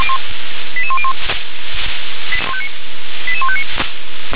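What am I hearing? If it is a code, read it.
Five-tone selective call (ZVEI), heard as quick runs of five tones, tones of about 70 ms. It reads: 88101 88159
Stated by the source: ZVEI five-tone call received on SDR.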